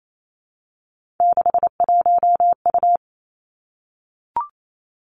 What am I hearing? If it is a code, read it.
Morse code, heard as a keyed tone, about 28 wpm, 700 Hz. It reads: 61U
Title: Three Challenging Letter-Number Combinations 28wpm